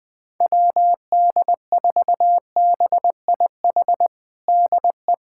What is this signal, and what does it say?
Morse code, heard as a keyed tone, about 20 wpm, 700 Hz. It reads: WD4BIH DE
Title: USA Callsign DE Callsign 20wpm